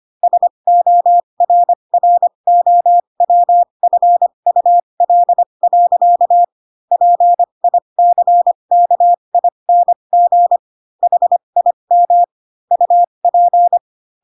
Morse code, heard as a keyed tone, about 25 wpm, spaced as for 22 wpm, 700 Hz. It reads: SORROWFUL. PICKING HIM UP
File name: Pinocchio / Pinocchio ch7 18 25 ik1hge